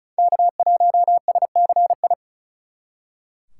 Morse code, heard as a keyed tone, about 39 wpm, 700 Hz. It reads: K1SCI
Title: Speed Racing - USA 1x3 Call Signs 35wpm